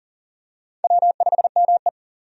Morse code, heard as a keyed tone, about 40 wpm, 700 Hz. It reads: W5ME
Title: USA Call Signs - 4 Characters - 40wpm